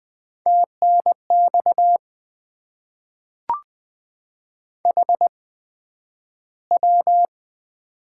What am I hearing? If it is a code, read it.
Morse code, heard as a keyed tone, about 20 wpm, 700 Hz. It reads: T N X H W
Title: Top 20 QSO Elements - 8x Character Spacing - 20wpm